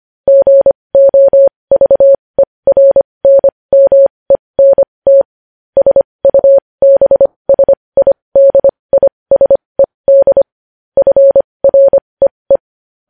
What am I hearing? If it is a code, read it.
Morse code, heard as a keyed tone, about 25 wpm, spaced as for 20 wpm, 550 Hz. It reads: GOVERNMENT SUBSIDISED FREE
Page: news-headlines-in-morse-code-at-20-wpm